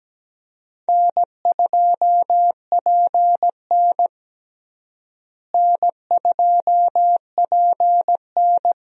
Morse code, heard as a keyed tone, about 17 wpm, 700 Hz. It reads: N2PN N2PN
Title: USA Call Signs - 4 Characters - Repeated 3x - 17wpm